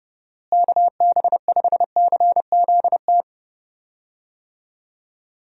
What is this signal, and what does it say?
Morse code, heard as a keyed tone, about 31 wpm, 700 Hz. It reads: KB5CZT
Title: USA Call Signs - 6 Characters - Repeated 3x - 30wpm